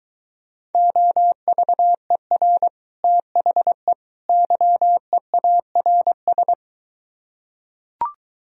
Morse code, heard as a keyed tone, about 23 wpm, 700 Hz. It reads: OVER THE YEARS